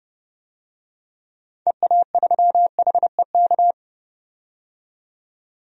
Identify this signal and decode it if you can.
Morse code, heard as a keyed tone, about 31 wpm, 700 Hz. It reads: EA3HEK
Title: DXCC-100 Call Signs - 6 Characters - 30wpm